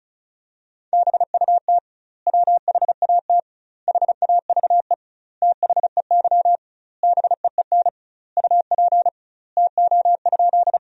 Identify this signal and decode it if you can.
Morse code, heard as a keyed tone, about 36 wpm, 700 Hz. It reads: BUT WHAT HAVE THEY BEEN UP TO?
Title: 2x Word-Spacing - Sentences from Top 100 Words - 35wpm